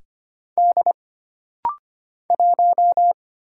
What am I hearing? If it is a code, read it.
Morse code, heard as a keyed tone, about 25 wpm, 700 Hz. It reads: D 1
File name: Single Character (Letter:Number:Punctuation:Prosign) - Rapid-Fire - 25wpm